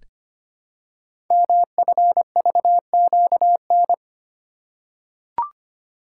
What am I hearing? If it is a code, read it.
Morse code, heard as a keyed tone, about 25 wpm, 700 Hz. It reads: MFVQN
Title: Code Groups - 5 Letters - 25wpm